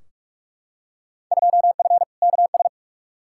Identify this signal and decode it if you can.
Morse code, heard as a keyed tone, about 46 wpm, 700 Hz. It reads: 2F KS
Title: Field Day Exchanges - 45wpm